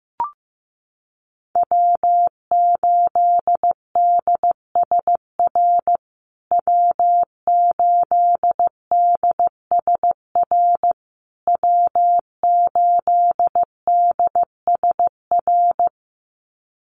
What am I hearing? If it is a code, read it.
Morse code, heard as a keyed tone, about 14 wpm, 700 Hz. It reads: W8DSR W8DSR W8DSR
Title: USA Call Signs Repeated 3x 15wpm